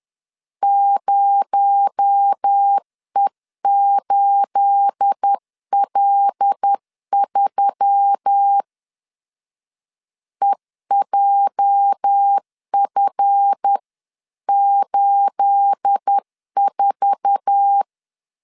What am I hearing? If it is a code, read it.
Morse code, heard as a keyed tone, about 10 wpm, 800 Hz. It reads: 0E8L3 EJF84
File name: LEZIONI / 9 28 10wpm 2842